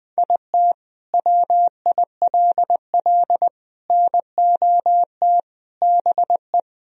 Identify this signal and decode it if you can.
Morse code, heard as a keyed tone, about 21 wpm, 700 Hz. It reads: IT WILL NOT BE
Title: Sentences from Top 1000 Words - Review Part 1 - 20wpm